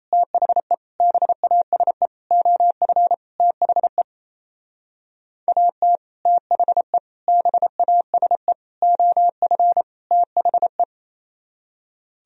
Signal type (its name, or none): Morse code